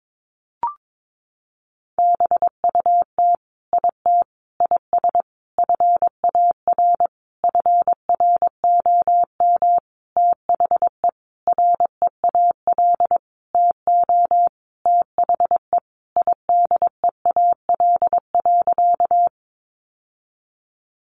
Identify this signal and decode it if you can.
Morse code, heard as a keyed tone, about 22 wpm, 700 Hz. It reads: BUT IT IS FAR FROM THE REAL TO THE IDEAL.
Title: Sentences from Top 1400 Words 22wpm